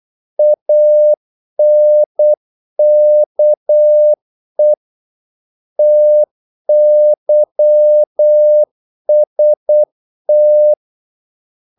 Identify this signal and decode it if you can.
Morse code, heard as a keyed tone, about 8 wpm, 600 Hz. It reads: ANKE TYST
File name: Korte ord DK 8wpm